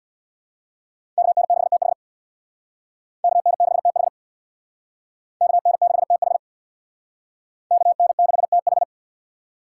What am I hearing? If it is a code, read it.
Morse code, heard as a keyed tone, about 65 wpm, 690 Hz. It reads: KN6TH KN6TH KN6TH KN6TH